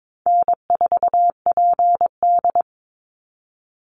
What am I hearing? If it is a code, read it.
Morse code, heard as a keyed tone, about 22 wpm, 700 Hz. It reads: N4PD
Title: Call Signs Repeated 2x - 22wpm